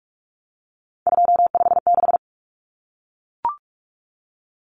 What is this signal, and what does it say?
Morse code, heard as a keyed tone, about 44 wpm, 700 Hz. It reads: 256
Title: Common Cut Numbers - 3-Digit 45wpm